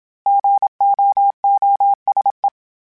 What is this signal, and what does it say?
Morse code, heard as a keyed tone, about 27 wpm, 800 Hz. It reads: GOOSE
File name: fromSMs / FromSMs mp3s / Cities@25